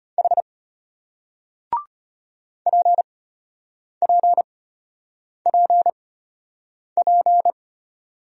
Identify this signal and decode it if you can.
Morse code, heard as a keyed tone, about 33 wpm, spaced as for 8 wpm, 700 Hz. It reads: H PPPP